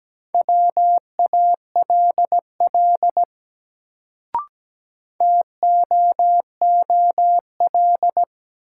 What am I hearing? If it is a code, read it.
Morse code, heard as a keyed tone, about 17 wpm, 700 Hz. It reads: WALL TOOL
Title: Sentences from Top 1000 Words - Review Part 6 - New Word Drill - 17wpm